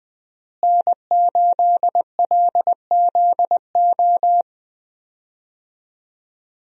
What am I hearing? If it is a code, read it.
Morse code, heard as a keyed tone, about 20 wpm, 700 Hz. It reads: N8LZO
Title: USA Call Signs - 5 Characters - Repeated 3x - 20wpm